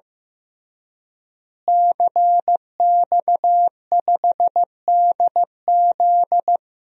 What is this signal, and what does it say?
Morse code, heard as a keyed tone, about 15 wpm, 700 Hz. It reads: CX5DZ
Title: DXCC-100 Call Signs - 5 Characters - Repeated 3x - 15wpm